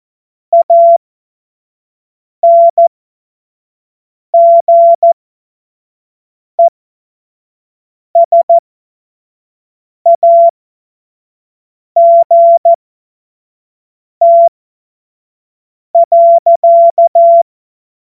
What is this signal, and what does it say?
Morse code, heard as a keyed tone, about 14 wpm, spaced as for 5 wpm, 680 Hz. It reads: ANGESAGT.
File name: Witz-16-September-2025-25-F70